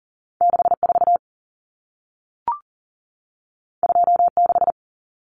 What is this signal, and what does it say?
Morse code, heard as a keyed tone, about 40 wpm, 700 Hz. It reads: 64 26
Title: Common Cut Numbers - 2-Digit 40wpm